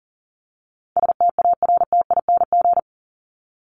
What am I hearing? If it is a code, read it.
Morse code, heard as a keyed tone, about 40 wpm, 700 Hz. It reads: STARTING
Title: ING Suffix - Top 100 Words with Base Words 40wpm